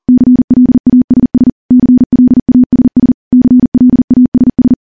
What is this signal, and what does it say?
Morse code, heard as a keyed tone, about 40 wpm, 250 Hz. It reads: CLASS CLASS CLASS
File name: morse / morse-training-40WPM